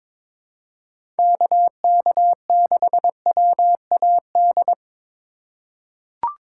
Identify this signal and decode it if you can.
Morse code, heard as a keyed tone, about 22 wpm, 700 Hz. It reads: KK6WAD
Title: USA Call Signs - 6 Characters - 2x Character Spacing - 22wpm